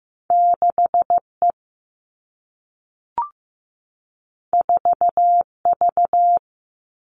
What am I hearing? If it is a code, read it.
Morse code, heard as a keyed tone, about 15 wpm, 700 Hz. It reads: 6E 4V
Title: All Cut Numbers - 2-Digit 15wpm